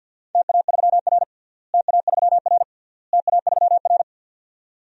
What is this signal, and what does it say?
Morse code, heard as a keyed tone, about 53 wpm, 700 Hz. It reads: TA3R TA3R TA3R
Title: DXCC-100 Call Signs - 4 Characters - Repeated 3x - 50wpm